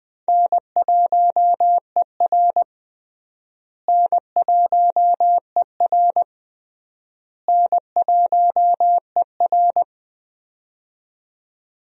Morse code, heard as a keyed tone, about 21 wpm, 700 Hz. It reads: N1ER N1ER N1ER
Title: USA Call Signs - 4 Characters - Repeated 3x - 20wpm